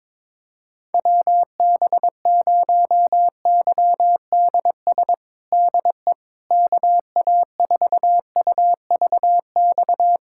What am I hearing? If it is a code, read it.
Morse code, heard as a keyed tone, about 21 wpm, 700 Hz. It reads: WB0YDS DE KA4UVX
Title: USA Callsign DE Callsign - 6 Characters - 22wpm